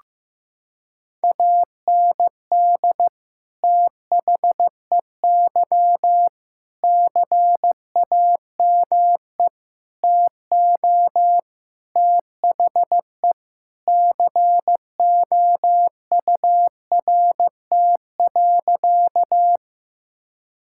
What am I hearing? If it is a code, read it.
Morse code, heard as a keyed tone, about 16 wpm, 700 Hz. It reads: AND THEY CAME TO THE COURT.
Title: Sentences from Top 1600 Words 15wpm